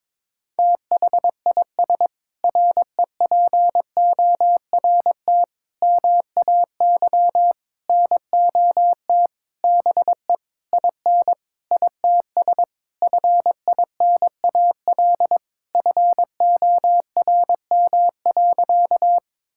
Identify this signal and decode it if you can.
Morse code, heard as a keyed tone, about 22 wpm, 700 Hz. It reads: THIS REPORT MAY NOT BE IN ITS FINAL FORM.
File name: Sentences with 7-Word Phrases - 22wpm